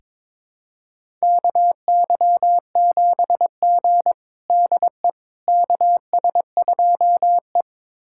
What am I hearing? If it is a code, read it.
Morse code, heard as a keyed tone, about 22 wpm, 700 Hz. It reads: KY7G DE KS2E
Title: USA Callsign DE Callsign - 4 Characters - 22wpm